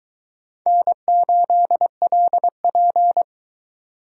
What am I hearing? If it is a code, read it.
Morse code, heard as a keyed tone, about 21 wpm, 700 Hz. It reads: N8LP